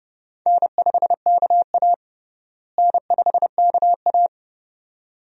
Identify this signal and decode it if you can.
Morse code, heard as a keyed tone, about 29 wpm, 700 Hz. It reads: N5KA N5KA
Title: USA Call Signs - 4 Characters - Repeated 2x - 30wpm